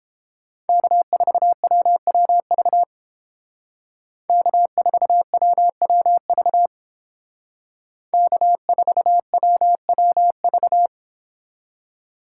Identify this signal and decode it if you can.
Morse code, heard as a keyed tone, about 30 wpm, 700 Hz. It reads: K4WWV K4WWV K4WWV